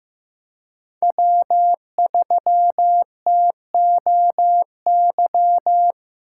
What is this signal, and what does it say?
Morse code, heard as a keyed tone, about 15 wpm, 700 Hz. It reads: W3TOY
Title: USA Call Signs - 4x Character Spacing - 15wpm